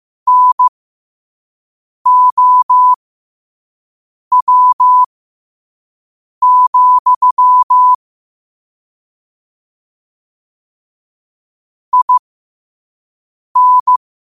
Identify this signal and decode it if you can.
Morse code, heard as a keyed tone, about 15 wpm, spaced as for 5 wpm, 1000 Hz. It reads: NOW, IN
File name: qotdfiles / 5wpm